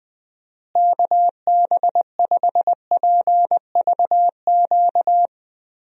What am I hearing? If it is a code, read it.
Morse code, heard as a keyed tone, about 21 wpm, 700 Hz. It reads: KB5PVQ
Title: USA Call Signs - 6 Characters - Repeated 3x - 20wpm